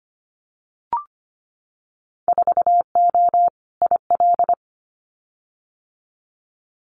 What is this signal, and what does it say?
Morse code, heard as a keyed tone, about 25 wpm, 700 Hz. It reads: 4O IL
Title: Winter Field Day Exchanges 25wpm